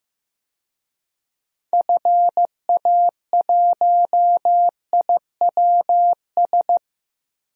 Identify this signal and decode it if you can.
Morse code, heard as a keyed tone, about 15 wpm, 700 Hz. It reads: FA1IWS
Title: DXCC-100 Call Signs - 6 Characters - Repeated 3x - 15wpm